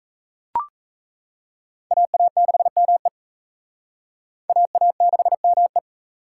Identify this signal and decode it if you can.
Morse code, heard as a keyed tone, about 35 wpm, 700 Hz. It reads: AA6ME AA6ME
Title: Speed Racing - USA 2x2 Call Signs 28wpm